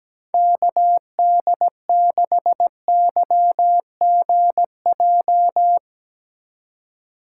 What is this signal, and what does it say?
Morse code, heard as a keyed tone, about 17 wpm, 700 Hz. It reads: KD6YGJ
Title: USA Call Signs - 6 Characters - 4x Character Spacing - 17wpm